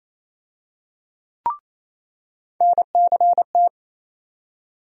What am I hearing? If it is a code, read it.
Morse code, heard as a keyed tone, about 28 wpm, 700 Hz. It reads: NCT
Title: USA Call Sign Postfixes - 3 Letters 28wpm